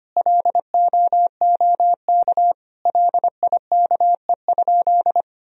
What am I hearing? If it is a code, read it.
Morse code, heard as a keyed tone, about 25 wpm, 700 Hz. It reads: LOOK LIKE?
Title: Sentences from Top 100 Words and Abbreviations 25wpm